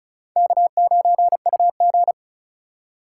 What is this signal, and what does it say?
Morse code, heard as a keyed tone, about 34 wpm, 700 Hz. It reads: K9UG
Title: USA Call Signs - 4 Characters - 2x Character Spacing - 35wpm